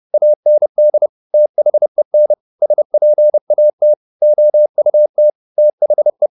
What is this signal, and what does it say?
Morse code, heard as a keyed tone, about 30 wpm, 600 Hz. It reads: AND THEN SPAT OUT THE
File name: the old man and the sea / the old man and the sea 1 30wpm